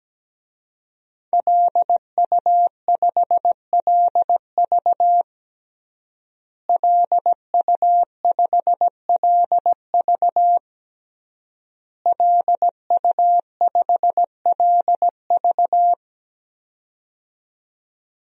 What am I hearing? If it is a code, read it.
Morse code, heard as a keyed tone, about 17 wpm, 700 Hz. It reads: LU5LV LU5LV LU5LV
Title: DXCC-100 Call Signs - 5 Characters - Repeated 3x - 17wpm